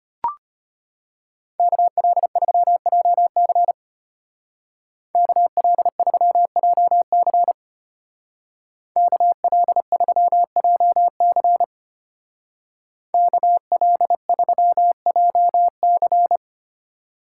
Morse code, heard as a keyed tone, about 32 wpm, 700 Hz. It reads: KL3JC KL3JC KL3JC KL3JC